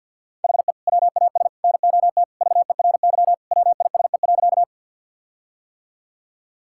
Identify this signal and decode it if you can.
Morse code, heard as a keyed tone, about 50 wpm, 700 Hz. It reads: HE WAS NOT VERY WISE.
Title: Sentences from Top 1600 Words 50wpm